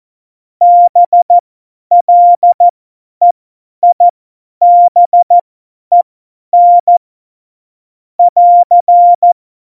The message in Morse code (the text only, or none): BLEIBEN <AR>